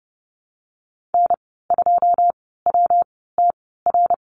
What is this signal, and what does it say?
Morse code, heard as a keyed tone, about 30 wpm, spaced as for 17 wpm, 700 Hz. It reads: N2WTR
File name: Lesson 19g - Letter w - Call Signs 30wpm